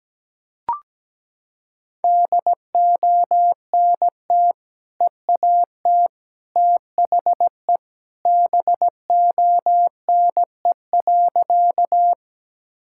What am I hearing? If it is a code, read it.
Morse code, heard as a keyed tone, about 17 wpm, 700 Hz. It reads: DONT EAT THE BONE.
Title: Sentences from Top 1600 Words 17wpm